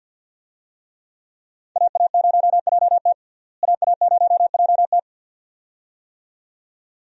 Morse code, heard as a keyed tone, about 51 wpm, 700 Hz. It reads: AA0JT AA0JT
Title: USA Call Signs - 5 Characters - Repeated 2x - 50wpm